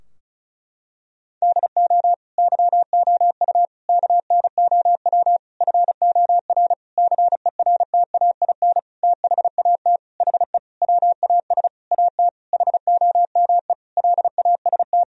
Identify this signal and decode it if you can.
Morse code, heard as a keyed tone, about 34 wpm, 700 Hz. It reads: DO YOU KNOW FOR CERTAIN THAT HE WAS AT HOME LAST
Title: Sentences from Top 1000 Words - Review Part 2 - 35wpm